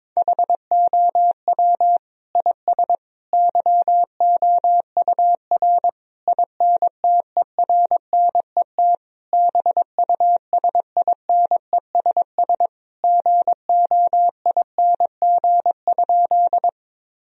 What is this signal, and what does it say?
Morse code, heard as a keyed tone, about 22 wpm, 700 Hz. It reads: HOW IS YOUR INTERNET BUSINESS GOING?
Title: Sentences from Top 1000 Words - Review Part 4 - 22wpm